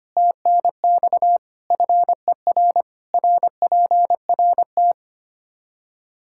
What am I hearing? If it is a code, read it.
Morse code, heard as a keyed tone, about 25 wpm, 700 Hz. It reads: TNX FER RPRT
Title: Practice QSO - Exchange 2 - Info - 25wpm